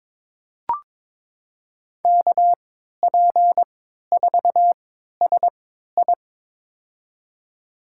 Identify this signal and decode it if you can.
Morse code, heard as a keyed tone, about 21 wpm, spaced as for 12 wpm, 700 Hz. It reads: KP4SI
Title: USA Call Signs - 3x Character Spacing - 22wpm